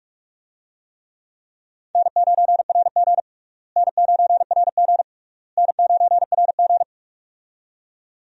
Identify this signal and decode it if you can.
Morse code, heard as a keyed tone, about 43 wpm, 700 Hz. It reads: N9RG N9RG N9RG
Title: USA Call Signs - 4 Characters - Repeated 3x - 45wpm